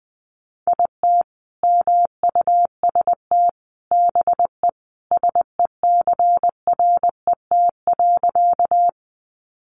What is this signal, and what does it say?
Morse code, heard as a keyed tone, about 20 wpm, 700 Hz. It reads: IT MUST BE SECRET.
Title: Sentences from Top 1200 Words 20wpm